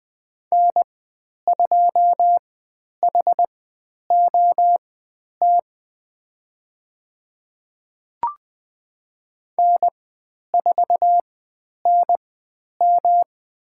Morse code, heard as a keyed tone, about 20 wpm, spaced as for 10 wpm, 700 Hz. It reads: N2HOT N4NM
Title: 109 Lesson 18g - Letter m - Call Signs 20wpm